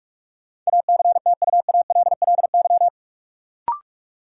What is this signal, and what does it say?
Morse code, heard as a keyed tone, about 46 wpm, 700 Hz. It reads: ACTUALLY
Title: Sentences from Top 700 Words - New Word Drill - 45wpm